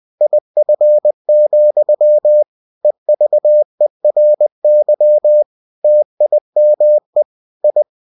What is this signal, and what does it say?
Morse code, heard as a keyed tone, about 20 wpm, 600 Hz. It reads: IF, EVERY TIME I
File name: cw-books / winnie-the-pooh / winnie-20wpm-page-0079